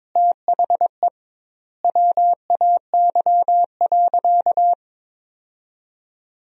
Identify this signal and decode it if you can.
Morse code, heard as a keyed tone, about 22 wpm, 700 Hz. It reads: THE WAY.
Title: 2x Word-Spacing - Sentences from Top 200 Words - 22wpm